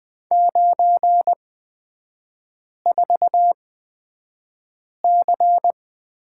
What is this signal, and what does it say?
Morse code, heard as a keyed tone, about 20 wpm, spaced as for 5 wpm, 700 Hz. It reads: 94C
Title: 243 Lesson 39e - Letter z - Three Characters 20wpm